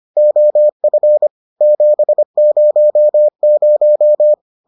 Morse code, heard as a keyed tone, about 25 wpm, 600 Hz. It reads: OF 700